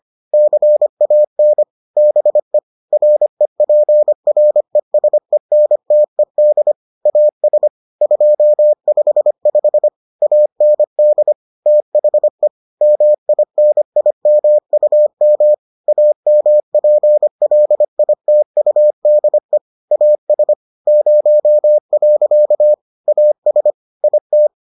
Morse code, heard as a keyed tone, about 25 wpm, 600 Hz. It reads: CAN BE REPRESENTED AS 255 AND THE MINIMUM AMPLITUDE AS 0. AS IT